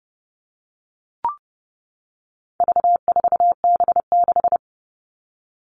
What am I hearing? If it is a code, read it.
Morse code, heard as a keyed tone, about 30 wpm, 700 Hz. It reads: V4B6
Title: All Cut Numbers - 4-Digit 30wpm